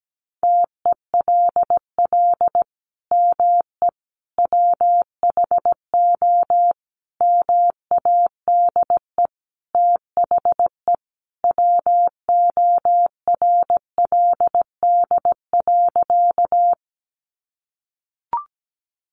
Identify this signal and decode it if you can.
Morse code, heard as a keyed tone, about 17 wpm, 700 Hz. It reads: TELL ME WHO MADE THE WORLD.
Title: Sentences from Top 200 Words 17wpm